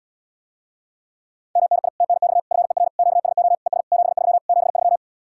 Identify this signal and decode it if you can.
Morse code, heard as a keyed tone, about 78 wpm, 700 Hz. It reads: MNI TNX FER QSO ES 73 73